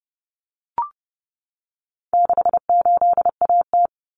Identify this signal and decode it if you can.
Morse code, heard as a keyed tone, about 30 wpm, 700 Hz. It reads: 68AT
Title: Common Cut Numbers - 4-Digit 30wpm